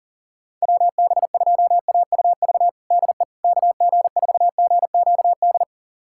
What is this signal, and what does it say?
Morse code, heard as a keyed tone, about 40 wpm, 700 Hz. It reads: WB2AUV DE KG4GQD
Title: USA Callsign DE Callsign - 6 Characters - 40wpm